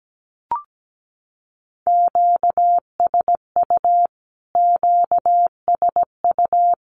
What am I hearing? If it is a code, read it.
Morse code, heard as a keyed tone, about 17 wpm, 700 Hz. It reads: QSU QSU
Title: Q-Codes Repeated 3x 17wpm